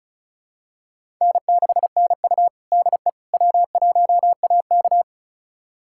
Morse code, heard as a keyed tone, about 36 wpm, 700 Hz. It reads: N6NU DE W1AK
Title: USA Callsign DE Callsign - 4 Characters - 35wpm